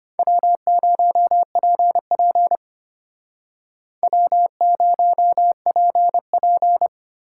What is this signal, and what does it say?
Morse code, heard as a keyed tone, about 32 wpm, 700 Hz. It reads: W0PP W0PP